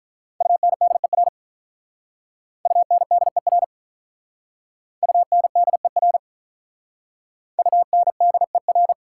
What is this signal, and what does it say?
Morse code, heard as a keyed tone, about 44 wpm, 690 Hz. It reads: UNDER UNDER UNDER UNDER